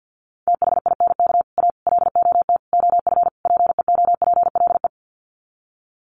Morse code, heard as a keyed tone, about 48 wpm, 700 Hz. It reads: THINK A LOT OF PEOPLE
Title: Sets of 5 Words 50wpm